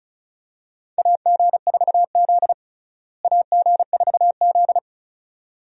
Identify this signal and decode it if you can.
Morse code, heard as a keyed tone, about 36 wpm, 700 Hz. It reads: AG4Z AG4Z
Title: USA Call Signs - 4 Characters - Repeated 2x - 35wpm